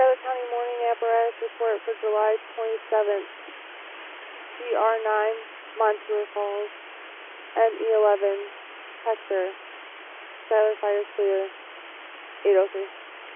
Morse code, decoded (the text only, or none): none